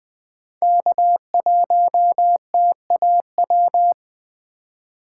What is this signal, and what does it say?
Morse code, heard as a keyed tone, about 20 wpm, 700 Hz. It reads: K1TAW
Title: USA Call Signs - 5 Characters - 3x Character Spacing - 20wpm